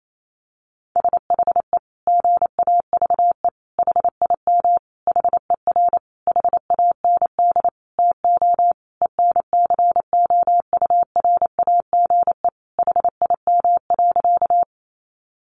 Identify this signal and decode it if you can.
Morse code, heard as a keyed tone, about 28 wpm, 700 Hz. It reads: SHE GAVE HIM HER HAND TO ENCOURAGE HIM.
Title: Sentences from Top 1400 Words 28wpm